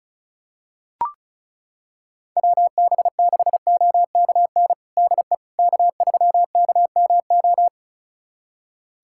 Morse code, heard as a keyed tone, about 36 wpm, 700 Hz. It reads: WB6OKN DE K3KMO
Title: USA Callsign DE Callsign 35wpm